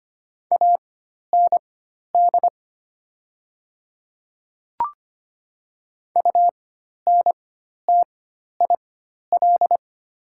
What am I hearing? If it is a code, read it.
Morse code, heard as a keyed tone, about 25 wpm, spaced as for 12 wpm, 690 Hz. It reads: AND UNTIL
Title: 4x Character Spacing - Top 500 Words - 25wpm